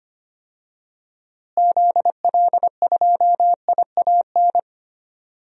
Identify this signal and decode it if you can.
Morse code, heard as a keyed tone, about 25 wpm, 700 Hz. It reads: ZL2IAN
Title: DXCC-100 Call Signs - 6 Characters - Repeated 3x - 25wpm